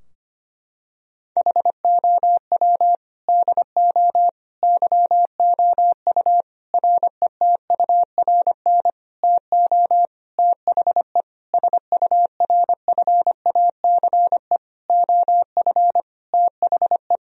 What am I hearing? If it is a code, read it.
Morse code, heard as a keyed tone, about 25 wpm, 700 Hz. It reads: HOW DO YOU RETURN TO THE SURFACE OF THE
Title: Sentences from Top 1000 Words - Review Part 3 - 25wpm